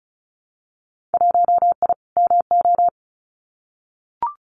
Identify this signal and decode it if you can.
Morse code, heard as a keyed tone, about 36 wpm, 700 Hz. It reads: 1I MO
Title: Winter Field Day Exchanges Encore 35wpm